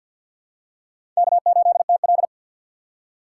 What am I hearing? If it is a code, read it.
Morse code, heard as a keyed tone, about 47 wpm, 700 Hz. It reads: K8TL
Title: USA Call Signs - 3x Character Spacing - 50wpm